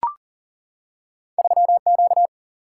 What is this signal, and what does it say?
Morse code, heard as a keyed tone, about 39 wpm, 700 Hz. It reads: 3Q
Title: Single-Letter Single-Number 40wpm